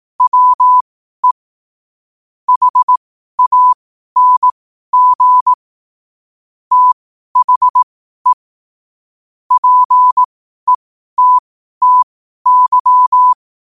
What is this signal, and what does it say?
Morse code, heard as a keyed tone, about 18 wpm, spaced as for 12 wpm, 1000 Hz. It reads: WE HANG THE PETTY
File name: qotdfiles / 12wpm